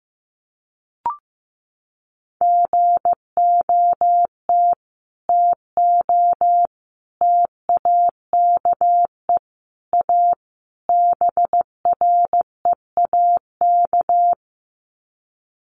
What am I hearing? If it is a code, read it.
Morse code, heard as a keyed tone, about 15 wpm, 700 Hz. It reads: GOT TO TAKE A BREAK
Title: Sets of 5 Words Encore 15wpm